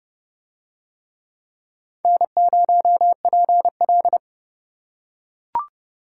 Morse code, heard as a keyed tone, about 29 wpm, 700 Hz. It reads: N0PL